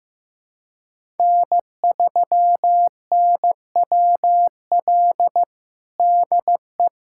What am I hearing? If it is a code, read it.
Morse code, heard as a keyed tone, about 15 wpm, 700 Hz. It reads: N3NWL DE
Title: USA Callsign DE Callsign - 5 Characters - 15wpm